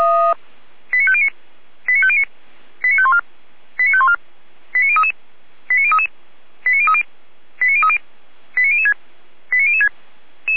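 Five-tone selective call (ZVEI), heard as quick runs of five tones, tones of about 70 ms. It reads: 89509 89509 89514 89514 89033 89033 89030 89030 89006 89006
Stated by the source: ZVEI five-tone call received by SDR.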